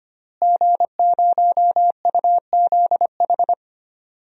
Morse code, heard as a keyed tone, about 25 wpm, 700 Hz. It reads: G0UZH
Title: DXCC-100 Call Signs - 5 Characters - Repeated 3x - 25wpm